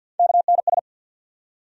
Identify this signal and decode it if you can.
Morse code, heard as a keyed tone, about 52 wpm, 700 Hz. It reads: KNS
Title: USA Call Sign Postfixes - 3 Letters 50wpm